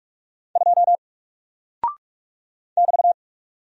Morse code, heard as a keyed tone, about 44 wpm, 700 Hz. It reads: 2 <BT>
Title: Single Character (Letter:Number:Punctuation:Prosign) 45wpm